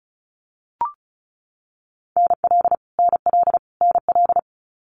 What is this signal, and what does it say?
Morse code, heard as a keyed tone, about 36 wpm, 700 Hz. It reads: NL NL NL
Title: Canadian Provinces and Territories - Repeated 3x - 35wpm